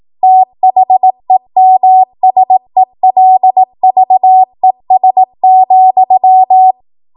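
Morse code, heard as a keyed tone, about 18 wpm, 750 Hz. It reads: THEMSELVES,
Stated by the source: Code practice at 18 WPM, a transition run bridging 15 to 20 WPM.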